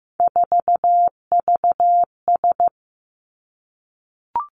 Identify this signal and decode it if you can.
Morse code, heard as a keyed tone, about 15 wpm, 700 Hz. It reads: 4VS
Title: Three Challenging Letter-Number Combinations 15wpm